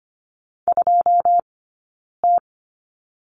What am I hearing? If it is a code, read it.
Morse code, heard as a keyed tone, about 25 wpm, 700 Hz. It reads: 2 T
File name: course / 25:10wpm / 092 Lesson 16d - Letter c - Two Characters 25wpm